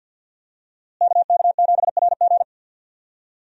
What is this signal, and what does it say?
Morse code, heard as a keyed tone, about 46 wpm, 700 Hz. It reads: KK7RG
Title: USA Call Signs - 5 Characters - Repeated 3x - 50wpm